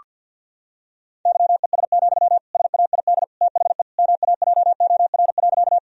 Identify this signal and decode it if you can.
Morse code, heard as a keyed tone, about 50 wpm, 700 Hz. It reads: YES, SAID THE MAJOR.
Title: Sentences from Top 1000 Words - Review Part 4 - 50wpm